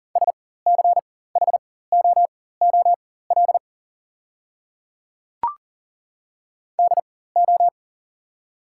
Morse code, heard as a keyed tone, about 40 wpm, spaced as for 19 wpm, 690 Hz. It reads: SCHOOL DO
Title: 4x Character Spacing - Top 300 Words - 40wpm